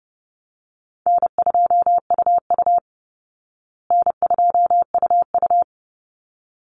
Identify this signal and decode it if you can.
Morse code, heard as a keyed tone, about 30 wpm, 700 Hz. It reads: N2UU N2UU
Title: Call Signs Repeated 3x - 30wpm